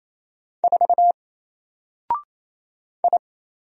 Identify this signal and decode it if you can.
Morse code, heard as a keyed tone, about 27 wpm, 700 Hz. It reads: 4 I
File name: Single-Letter-Number-28wpm